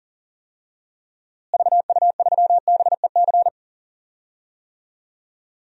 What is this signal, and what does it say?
Morse code, heard as a keyed tone, about 38 wpm, 700 Hz. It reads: VU3BEC
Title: DXCC-100 Call Signs - 6 Characters - Repeated 2x - 40wpm